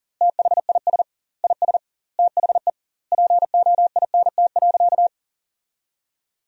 Morse code, heard as a keyed tone, about 40 wpm, 700 Hz. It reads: THIS IS THE POINT.
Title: 2x Word-Spacing - Sentences from Top 200 Words - 40wpm